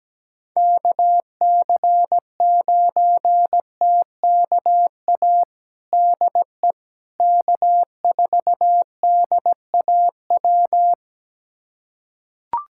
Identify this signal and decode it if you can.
Morse code, heard as a keyed tone, about 17 wpm, 700 Hz. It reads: KC9TKA DE K4DAW
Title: USA Callsign DE Callsign 17wpm